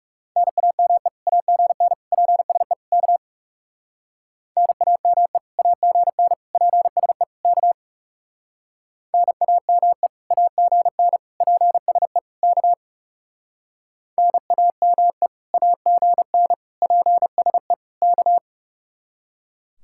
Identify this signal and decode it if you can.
Morse code, heard as a keyed tone, about 36 wpm, 700 Hz. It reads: NAME AGN PSE K NAME AGN PSE K NAME AGN PSE K NAME AGN PSE K
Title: Speed-Racing Common QSO Phrases - 30wpm